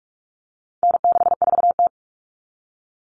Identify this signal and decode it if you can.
Morse code, heard as a keyed tone, about 44 wpm, 700 Hz. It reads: N64T
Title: Common Cut Numbers - 4-Digit 45wpm